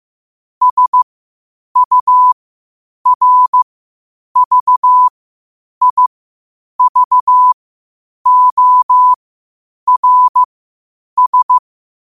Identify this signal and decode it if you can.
Morse code, heard as a keyed tone, about 15 wpm, spaced as for 8 wpm, 1000 Hz. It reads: SURVIVORS
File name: qotdfiles / 8wpm